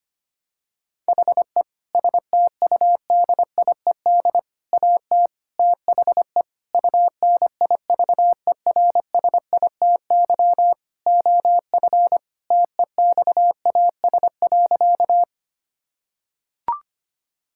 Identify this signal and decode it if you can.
Morse code, heard as a keyed tone, about 25 wpm, 700 Hz. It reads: HE STUDIED AT THE UNIVERSITY OF TEXAS.
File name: practice / 25wpm / Sentences with 4-Word Phrases - 25wpm